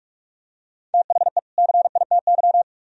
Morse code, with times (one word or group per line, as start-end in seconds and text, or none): 0.9-1.4 THE
1.6-2.6 CITY